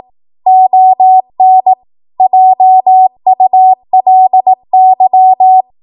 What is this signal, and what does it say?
Morse code, heard as a keyed tone, about 18 wpm, 750 Hz. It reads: ON JULY
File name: Archive / 18 WPM / 18WPM